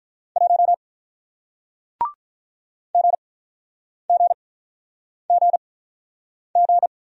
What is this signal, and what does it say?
Morse code, heard as a keyed tone, about 48 wpm, spaced as for 9 wpm, 700 Hz. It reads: 1 GGGG